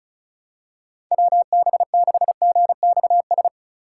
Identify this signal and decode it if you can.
Morse code, heard as a keyed tone, about 36 wpm, 700 Hz. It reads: WB6GXS
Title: USA Call Signs - 6 Characters - 4x Character Spacing - 35wpm